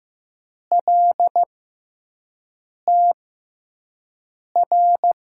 Morse code, heard as a keyed tone, about 15 wpm, spaced as for 5 wpm, 700 Hz. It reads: LTR
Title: USA Call Signs - 6 Characters - 6x Character Spacing - 15wpm